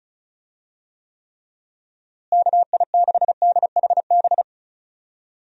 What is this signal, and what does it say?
Morse code, heard as a keyed tone, about 34 wpm, 700 Hz. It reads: KI6DHB
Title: USA Call Signs - 6 Characters - Repeated 3x - 35wpm